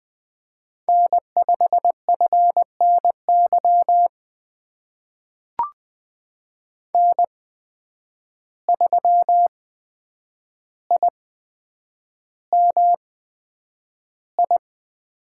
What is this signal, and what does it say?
Morse code, heard as a keyed tone, about 20 wpm, spaced as for 9 wpm, 700 Hz. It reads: N5FNY N3IMI
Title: USA Call Signs - 5 Characters - 8x Character Spacing - 20wpm